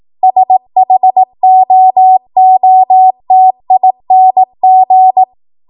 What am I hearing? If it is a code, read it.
Morse code, heard as a keyed tone, about 18 wpm, 750 Hz. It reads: SHOOTING
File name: Archive / 18 WPM / 18WPM